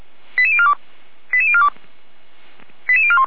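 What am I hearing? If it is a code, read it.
Five-tone selective call (ZVEI), heard as quick runs of five tones, tones of about 70 ms. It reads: 88042 88042 88041
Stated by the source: ZVEI five-tone selective call received by SDR, from audio.f85360.feuerwehr-orlamuende.org (Feuerwehr Orlamünde fire brigade).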